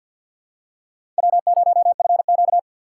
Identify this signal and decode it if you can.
Morse code, heard as a keyed tone, about 46 wpm, 700 Hz. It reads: W0FQ
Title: USA Call Signs - 4 Characters - 6x Character Spacing - 50wpm